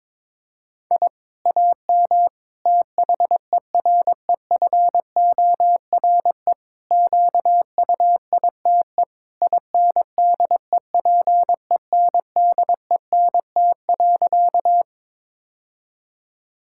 Morse code, heard as a keyed tone, about 22 wpm, 700 Hz. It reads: I AM THEREFORE QUITE INDEPENDENT.
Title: Sentences from Top 1000 Words - Review Part 6 - 22wpm